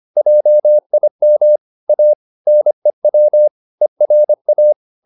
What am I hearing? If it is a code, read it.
Morse code, heard as a keyed tone, about 25 wpm, 600 Hz. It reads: JIM A NEW ERA